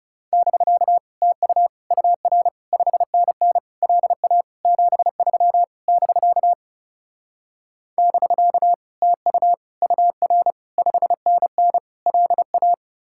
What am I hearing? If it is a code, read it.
Morse code, heard as a keyed tone, about 41 wpm, 700 Hz. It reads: <BK> TU UR 5NN LA 73 <BK> <BK> TU UR 5NN LA